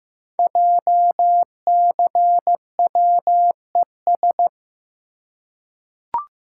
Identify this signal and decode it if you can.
Morse code, heard as a keyed tone, about 15 wpm, 700 Hz. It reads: JCWES
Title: Code Groups - 5 Letters - 15wpm